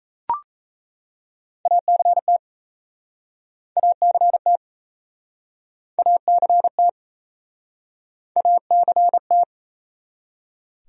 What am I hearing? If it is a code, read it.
Morse code, heard as a keyed tone, about 38 wpm, 700 Hz. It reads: ACT ACT ACT ACT